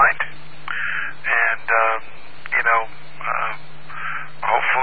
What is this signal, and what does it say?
Emergency Alert System (SAME) tones, heard as harsh buzzing bursts, with speech. It is heard from 0.7 to 4.3 s.